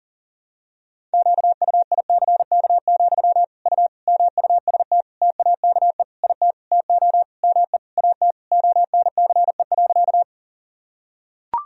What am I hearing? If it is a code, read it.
Morse code, heard as a keyed tone, about 40 wpm, 700 Hz. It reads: QUICK, U MUST TAKE IT TO ME AT ONCE.
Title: Sentences from Top 500 Words and Abbreviations 40wpm